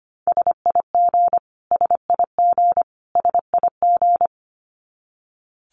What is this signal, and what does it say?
Morse code, heard as a keyed tone, about 25 wpm, 700 Hz. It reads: SIG SIG SIG
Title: Abbreviations Repeated 3x - 25wpm